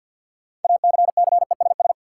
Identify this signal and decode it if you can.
Morse code, heard as a keyed tone, about 53 wpm, 700 Hz. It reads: ACCESS
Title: Sentences from Top 1000 Words - Review Part 4 - New Word Drill - 50wpm